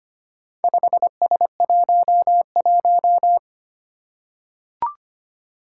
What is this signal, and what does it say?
Morse code, heard as a keyed tone, about 25 wpm, 700 Hz. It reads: 5S11
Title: Four Challenging Letter-Number Combinations 25wpm